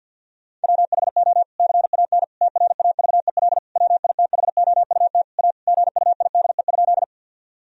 Morse code, heard as a keyed tone, about 50 wpm, 700 Hz. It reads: WHO CAN TRAVEL WITHOUT A GUIDE?
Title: Sentences from Top 1000 Words - Review Part 3 - 50wpm